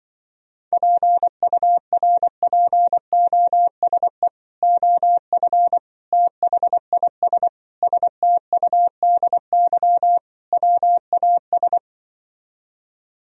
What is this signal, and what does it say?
Morse code, heard as a keyed tone, about 24 wpm, 700 Hz. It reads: PURPOSE OF THIS STUDY WAS
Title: Speed-Racing - Sets of 5 Words 20wpm